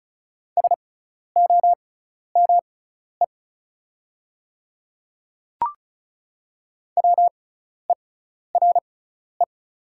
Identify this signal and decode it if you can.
Morse code, heard as a keyed tone, about 36 wpm, spaced as for 12 wpm, 700 Hz. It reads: SOME WERE
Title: 6x Character Spacing - Top 200 Words - 35wpm